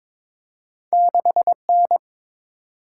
Morse code, heard as a keyed tone, about 22 wpm, 700 Hz. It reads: 6N
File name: practice / 22wpm / Two Letter-Number Combinations - 22wpm